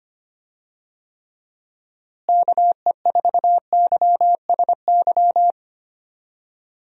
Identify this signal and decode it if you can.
Morse code, heard as a keyed tone, about 25 wpm, 700 Hz. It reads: KE4YSY
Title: USA Call Signs - 6 Characters - Repeated 2x - 25wpm